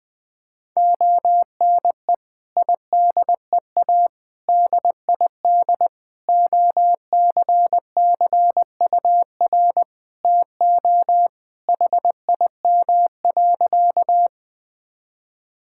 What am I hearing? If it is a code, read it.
Morse code, heard as a keyed tone, about 21 wpm, 700 Hz. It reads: ONE IDEA DID OCCUR TO HIM.
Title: Sentences from Top 1000 Words - Review Part 6 - 20wpm